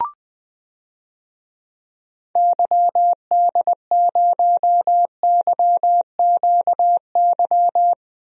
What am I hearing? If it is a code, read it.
Morse code, heard as a keyed tone, about 21 wpm, 700 Hz. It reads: YD0YQY
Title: DXCC-100 Call Signs - 6 Characters - Repeated 2x - 20wpm